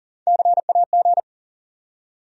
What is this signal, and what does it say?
Morse code, heard as a keyed tone, about 40 wpm, 700 Hz. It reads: CAG
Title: USA Call Sign Postfixes - 3 Letters 40wpm